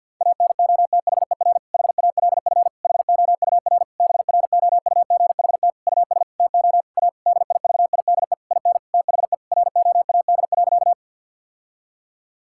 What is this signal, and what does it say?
Morse code, heard as a keyed tone, about 48 wpm, 700 Hz. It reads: ANOTHER HALF HOUR BROUGHT US TO A DIVIDE IN THE ROAD.
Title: Sentences from Top 1600 Words 50wpm